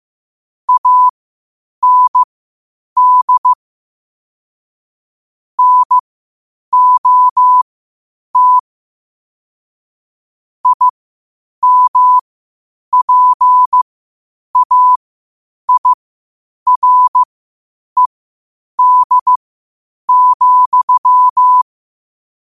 Morse code, heard as a keyed tone, about 15 wpm, spaced as for 8 wpm, 1000 Hz. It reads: AND NOT IMPAIRED,
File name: qotdfiles / 8wpm